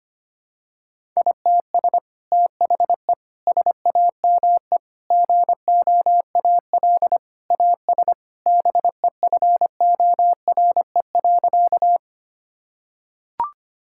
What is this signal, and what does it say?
Morse code, heard as a keyed tone, about 25 wpm, 700 Hz. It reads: ITS THE SAME GOAL AS BEFORE.
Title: Sentences from Top 1000 Words - Review Part 5 - 25wpm